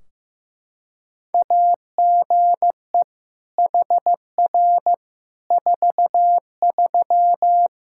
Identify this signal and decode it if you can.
Morse code, heard as a keyed tone, about 15 wpm, 700 Hz. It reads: AGE HR 43
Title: Practice QSO - Exchange 3 - More Info - 15wpm